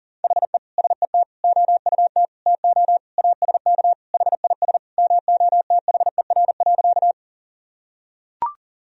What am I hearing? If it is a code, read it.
Morse code, heard as a keyed tone, about 40 wpm, 700 Hz. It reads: HE SET OUT TO ASK HIS MOTHER.
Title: Sentences from Top 1000 Words - Review Part 1 - 40wpm